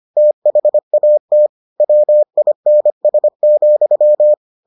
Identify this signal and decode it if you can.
Morse code, heard as a keyed tone, about 25 wpm, 600 Hz. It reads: THAT WINS,